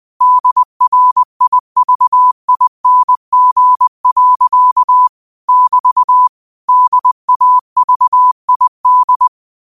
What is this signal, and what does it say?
Morse code, heard as a keyed tone, about 20 wpm, 1000 Hz. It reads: DRIVING. <BT> DAVID